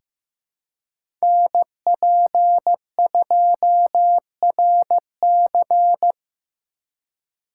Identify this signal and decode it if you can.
Morse code, heard as a keyed tone, about 15 wpm, 700 Hz. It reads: NP2RC
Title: USA Call Signs - 5 Characters - 4x Character Spacing - 15wpm